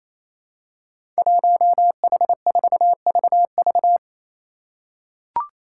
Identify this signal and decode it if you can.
Morse code, heard as a keyed tone, about 28 wpm, 700 Hz. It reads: 1H4VV
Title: Five Challenging Letter-Number Combinations 28wpm